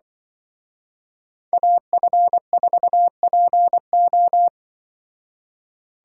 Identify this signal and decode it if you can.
Morse code, heard as a keyed tone, about 24 wpm, 700 Hz. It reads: AF4PO
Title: Speed Racing - USA Call Signs 20wpm